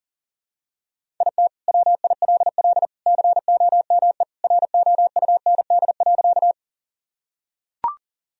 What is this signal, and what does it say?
Morse code, heard as a keyed tone, about 37 wpm, 700 Hz. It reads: IT WILL COME ROUND.
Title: Sentences from Top 1000 Words - Review Part 1 - 40wpm